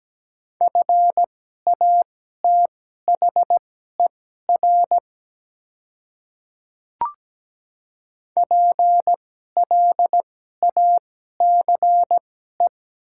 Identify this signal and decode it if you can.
Morse code, heard as a keyed tone, about 17 wpm, spaced as for 12 wpm, 700 Hz. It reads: FATHER PLACE
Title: 2x Character Spacing - Top 300 Words - 17wpm